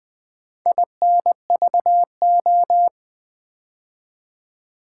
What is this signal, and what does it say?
Morse code, heard as a keyed tone, about 20 wpm, 700 Hz. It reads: INVO
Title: Build A Word - 4 Letters At A Time - 11-Letter Words - 20wpm